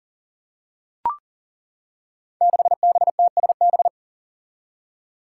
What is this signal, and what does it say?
Morse code, heard as a keyed tone, about 42 wpm, 700 Hz. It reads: 6BTSB
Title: Five Challenging Letter-Number Combinations 40wpm